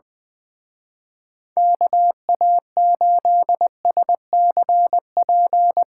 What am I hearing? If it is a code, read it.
Morse code, heard as a keyed tone, about 22 wpm, 700 Hz. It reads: KA8SCP
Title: Speed Racing - USA 2x3 Call Signs 20wpm